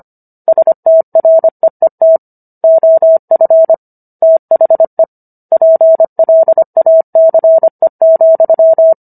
Morse code, heard as a keyed tone, about 25 wpm, 650 Hz. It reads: STREET OF THE PLACE,